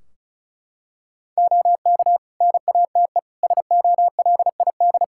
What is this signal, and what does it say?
Morse code, heard as a keyed tone, about 34 wpm, 700 Hz. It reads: OK NATE SOLID
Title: Practice QSO - Exchange 3 - More Info - 35wpm